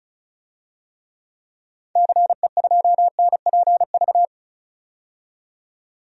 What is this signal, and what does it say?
Morse code, heard as a keyed tone, about 36 wpm, 700 Hz. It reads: CE2NPV
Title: DXCC-100 Call Signs - 6 Characters - 35wpm